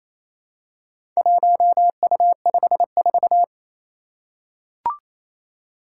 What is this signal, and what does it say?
Morse code, heard as a keyed tone, about 27 wpm, 700 Hz. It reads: 1U54
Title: Four Challenging Letter-Number Combinations 28wpm